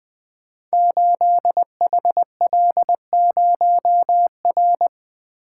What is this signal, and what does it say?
Morse code, heard as a keyed tone, about 20 wpm, 700 Hz. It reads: 8HL0R
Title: Code Groups - 5 Letter:Numbers - 20wpm